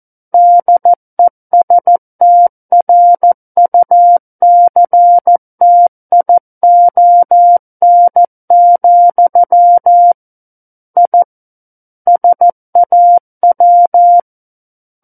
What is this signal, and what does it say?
Morse code, heard as a keyed tone, about 14 wpm, 700 Hz. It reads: DESTRUCTION, I SAW
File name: Book One / 05-Chapter 5 - 14 WPM